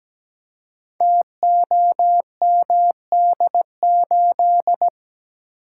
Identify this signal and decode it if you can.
Morse code, heard as a keyed tone, about 17 wpm, 700 Hz. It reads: TOMD8
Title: Code Groups - 5 Letter:Numbers - 17wpm